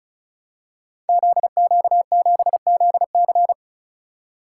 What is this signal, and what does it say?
Morse code, heard as a keyed tone, about 34 wpm, 700 Hz. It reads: ZQ7ZC
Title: Code Groups - 5 Letter:Numbers - 35wpm